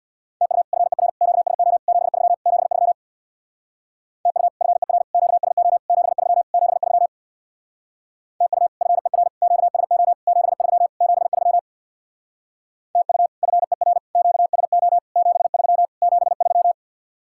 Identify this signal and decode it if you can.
Morse code, heard as a keyed tone, about 64 wpm, 700 Hz. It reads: TU FER QSO 73 73 TU FER QSO 73 73 TU FER QSO 73 73 TU FER QSO 73 73